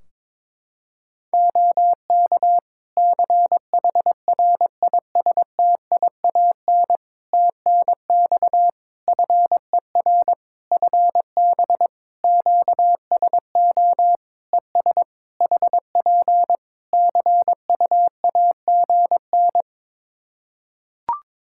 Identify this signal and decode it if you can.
Morse code, heard as a keyed tone, about 22 wpm, 700 Hz. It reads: OK CHRISTIAN TNX FER FB QSO ES HP CUAGN
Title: Practice QSO - Exchange 4 - Ending - 22wpm